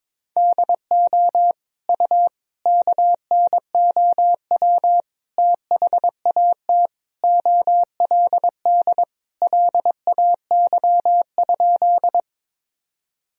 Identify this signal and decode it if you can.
Morse code, heard as a keyed tone, about 21 wpm, 700 Hz. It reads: DO U KNOW THAT OLD LAY?
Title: Sentences from Top 400 Words and Abbreviations 22wpm